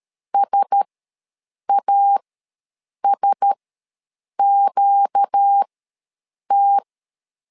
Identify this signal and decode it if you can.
Morse code, heard as a keyed tone, about 13 wpm, spaced as for 7 wpm, 800 Hz. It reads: SASQT
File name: LEZIONI / 4 Q1S ws 12wpm 1827